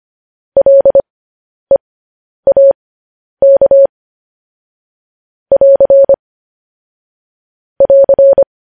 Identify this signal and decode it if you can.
Morse code, heard as a keyed tone, about 25 wpm, spaced as for 10 wpm, 550 Hz. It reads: LEAK <AR> <AR>
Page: news-headlines-in-morse-code-at-10-wpm